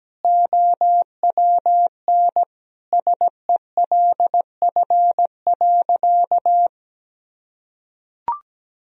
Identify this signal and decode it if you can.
Morse code, heard as a keyed tone, about 17 wpm, 700 Hz. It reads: OWN SELF.
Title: Sentences from Top 1000 Words - Review Part 1 - 17wpm